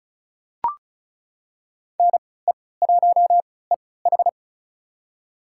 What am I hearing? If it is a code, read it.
Morse code, heard as a keyed tone, about 36 wpm, spaced as for 20 wpm, 700 Hz. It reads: NE1EH
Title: USA Call Signs - 3x Character Spacing - 35wpm